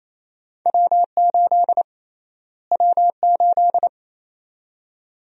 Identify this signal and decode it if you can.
Morse code, heard as a keyed tone, about 28 wpm, 700 Hz. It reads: W8 W8
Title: USA Call Sign Prefixes - Repeated 2x - 28wpm